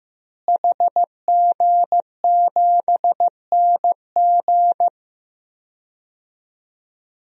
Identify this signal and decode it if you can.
Morse code, heard as a keyed tone, about 15 wpm, 700 Hz. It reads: HG7NG
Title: DXCC-100 Call Signs - 5 Characters - Repeated 2x - 15wpm